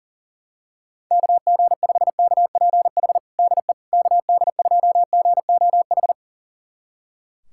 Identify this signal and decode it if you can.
Morse code, heard as a keyed tone, about 40 wpm, 700 Hz. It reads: KG5KPH DE KD2GOH
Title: USA Callsign DE Callsign 40wpm